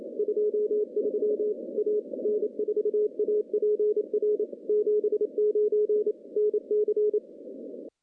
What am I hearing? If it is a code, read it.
Morse code, heard as a keyed tone, about 28 wpm, 420 Hz. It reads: 23 A N4APR 79 NC
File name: contests / CW-SS-k5tr- -0200